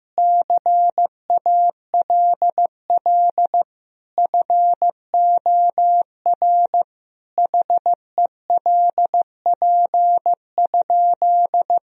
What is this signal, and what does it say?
Morse code, heard as a keyed tone, about 15 wpm, 700 Hz. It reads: CALL FOR HELP?
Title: Sentences from Top 1000 Words - Review Part 4 - 15wpm